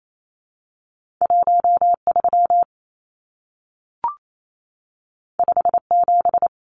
Morse code, heard as a keyed tone, about 28 wpm, 700 Hz. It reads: 13 57
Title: Numbers - 2-Digit 28wpm